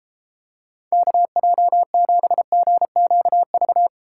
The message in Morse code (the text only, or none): KJ7GQV